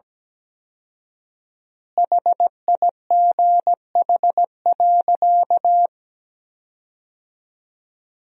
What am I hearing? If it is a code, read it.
Morse code, heard as a keyed tone, about 17 wpm, 700 Hz. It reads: HIGH.
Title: 4x Word-Spacing - Sentences from Top 200 Words - 17wpm